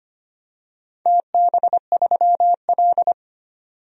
Morse code, heard as a keyed tone, about 25 wpm, 700 Hz. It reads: TB3L
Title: DXCC-100 Call Signs - 4 Characters - 25wpm